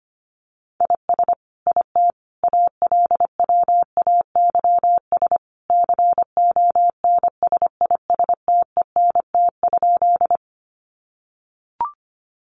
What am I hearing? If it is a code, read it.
Morse code, heard as a keyed tone, about 25 wpm, 700 Hz. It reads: IS IT ALWAYS CONSISTENT?
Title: Sentences from Top 1500 Words 25wpm